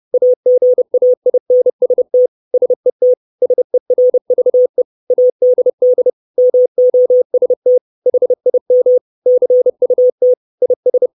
Morse code, heard as a keyed tone, about 30 wpm, 500 Hz. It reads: AGAINST SET SERVE ADD MOST HIM CUT IS
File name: 632 Words 30WPM